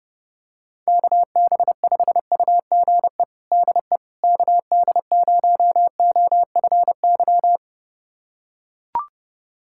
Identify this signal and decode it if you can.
Morse code, heard as a keyed tone, about 30 wpm, 700 Hz. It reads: KB5UGE DE KD0OFY
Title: USA Callsign DE Callsign 30wpm